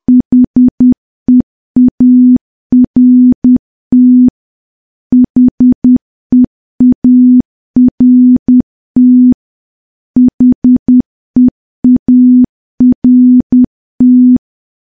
Morse code, heard as a keyed tone, about 10 wpm, 250 Hz. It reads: HEART HEART HEART
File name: morse / morse-training-10WPM